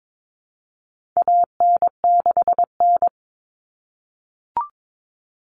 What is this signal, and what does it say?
Morse code, heard as a keyed tone, about 22 wpm, 700 Hz. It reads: AN6N
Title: Common Cut Numbers - 4-Digit 22wpm